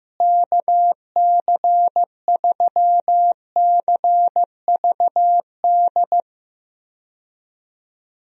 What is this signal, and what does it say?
Morse code, heard as a keyed tone, about 15 wpm, 700 Hz. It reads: KC3CVD
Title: USA Call Signs - 6 Characters - Repeated 3x - 15wpm